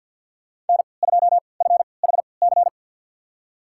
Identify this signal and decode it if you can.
Morse code, heard as a keyed tone, about 52 wpm, spaced as for 29 wpm, 700 Hz. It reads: N2FHC
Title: USA Call Signs - 5 Characters - 3x Character Spacing - 50wpm